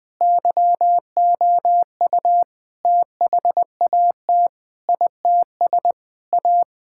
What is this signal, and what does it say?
Morse code, heard as a keyed tone, about 20 wpm, 700 Hz. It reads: YOU THAT ITS A
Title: Sentences with 5-Word Phrases - Spoken Beginning - 20wpm